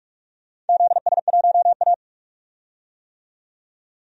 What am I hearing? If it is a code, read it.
Morse code, heard as a keyed tone, about 46 wpm, 700 Hz. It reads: ZS1A
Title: DXCC-100 Call Signs - 4 Characters - 45wpm